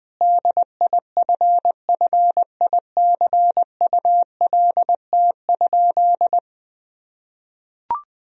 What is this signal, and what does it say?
Morse code, heard as a keyed tone, about 20 wpm, 700 Hz. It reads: DIFFICULT?
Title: Sentences from Top 1000 Words - Review Part 4 - 20wpm